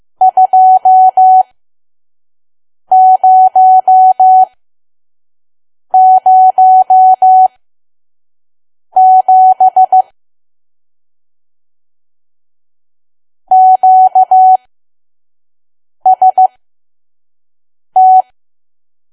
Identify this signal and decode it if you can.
Morse code, heard as a keyed tone, about 15 wpm, spaced as for 5 wpm, 750 Hz. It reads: 2007 QST
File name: Archive / 5 WPM files / PM